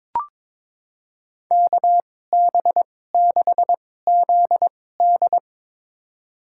Morse code, heard as a keyed tone, about 22 wpm, spaced as for 16 wpm, 700 Hz. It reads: KB6ZD
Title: USA Call Signs - 5 Characters - 2x Character Spacing - 22wpm